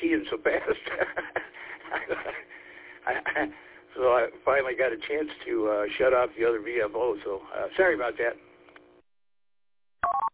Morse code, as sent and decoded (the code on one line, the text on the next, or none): none